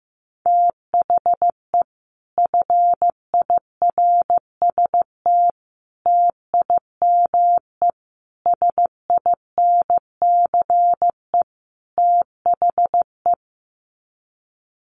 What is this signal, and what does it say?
Morse code, heard as a keyed tone, about 15 wpm, 700 Hz. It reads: THE FIRST TIME SINCE THE
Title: Sets of 5 Words Encore 15wpm